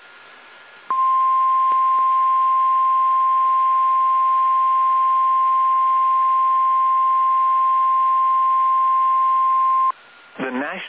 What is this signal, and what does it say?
Emergency Alert System (SAME) tones (the 1050 Hz attention tone), with speech. It is heard from 0.9 to 10.0 s.
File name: eas / KSDZKDJL / KSDZKDJL